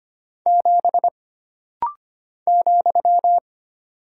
Morse code, heard as a keyed tone, about 25 wpm, 700 Hz. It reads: 7 ,
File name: Single Character (Letter:Number:Punctuation:Prosign) - Rapid-Fire - 25wpm